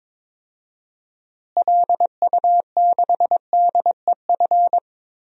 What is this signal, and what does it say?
Morse code, heard as a keyed tone, about 22 wpm, 700 Hz. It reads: LU6DEF
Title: DXCC-100 Call Signs - 6 Characters - Repeated 2x - 22wpm